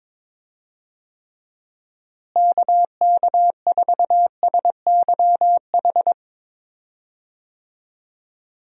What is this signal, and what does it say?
Morse code, heard as a keyed tone, about 22 wpm, 700 Hz. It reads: KK4SYH
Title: USA Call Signs - 6 Characters - Repeated 3x - 22wpm